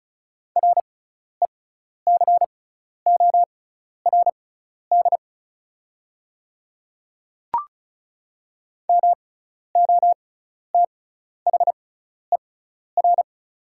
Morse code, heard as a keyed tone, about 36 wpm, spaced as for 12 wpm, 700 Hz. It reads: RECORD MOTHER
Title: 6x Character Spacing - Top 500 Words - 35wpm